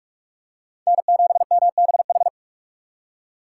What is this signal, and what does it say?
Morse code, heard as a keyed tone, about 42 wpm, 690 Hz. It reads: N7MBH
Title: USA Call Signs - 5 Characters - 2x Character Spacing - 45wpm